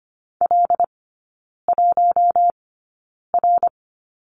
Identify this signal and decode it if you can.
Morse code, heard as a keyed tone, about 25 wpm, spaced as for 9 wpm, 690 Hz. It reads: L1R
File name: Lesson 15e - Number 5 - Three Characters 25wpm